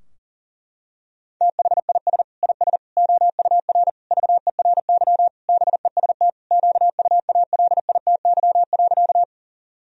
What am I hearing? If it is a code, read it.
Morse code, heard as a keyed tone, about 37 wpm, 700 Hz. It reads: THIS IS OUR VERY BEST QUALITY.
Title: Sentences from Top 1000 Words - Review Part 4 - 40wpm